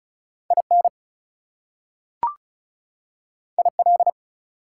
Morse code, heard as a keyed tone, about 36 wpm, 700 Hz. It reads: IN IL
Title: US States and Canadian Provinces 35wpm